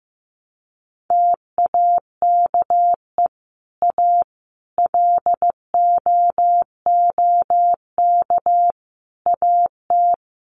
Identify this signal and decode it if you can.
Morse code, heard as a keyed tone, about 15 wpm, 700 Hz. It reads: TAKE A LOOK AT
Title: Sets of 5 Words 15wpm